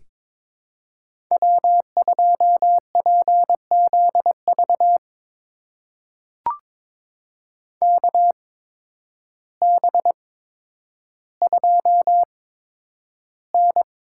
Morse code, heard as a keyed tone, about 22 wpm, spaced as for 10 wpm, 700 Hz. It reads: W2PZV KB2N
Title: USA Call Signs - 8x Character Spacing - 22wpm